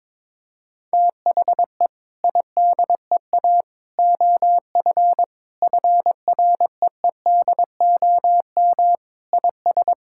Morse code, heard as a keyed tone, about 22 wpm, 700 Hz. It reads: THE IDEA OF FREEDOM IS
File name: practice / 22wpm / Sentences with 3-Word Phrases - 22wpm